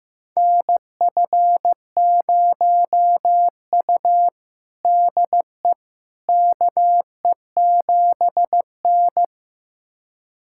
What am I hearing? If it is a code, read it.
Morse code, heard as a keyed tone, about 15 wpm, 700 Hz. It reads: NF0U DE KE7N
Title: USA Callsign DE Callsign - 4 Characters - 15wpm